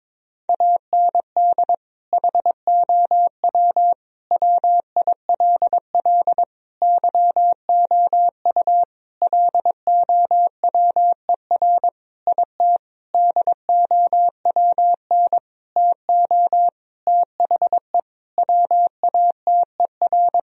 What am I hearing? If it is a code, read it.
Morse code, heard as a keyed tone, about 22 wpm, 700 Hz. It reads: AND HOW WILL YOU LOWER IT DOWN TO THE WATER
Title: Sentences from Top 1000 Words - Review Part 4 - 22wpm